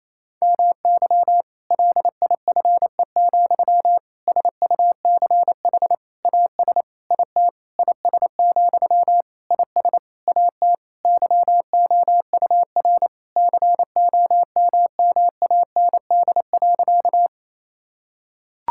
Morse code, heard as a keyed tone, about 27 wpm, 700 Hz. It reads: MY LIFE, SUCH AS IT IS, IS AT YOUR COMMAND.
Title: Sentences from Top 1600 Words 28wpm